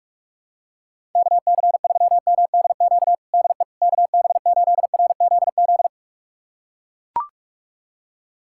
Morse code, heard as a keyed tone, about 43 wpm, 700 Hz. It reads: KC3MDQ DE KB8RZZ
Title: USA Callsign DE Callsign 45wpm